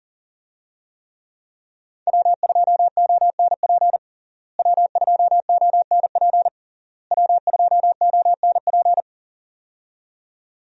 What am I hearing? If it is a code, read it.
Morse code, heard as a keyed tone, about 40 wpm, 700 Hz. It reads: W2ONP W2ONP W2ONP
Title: USA Call Signs - 5 Characters - Repeated 3x - 40wpm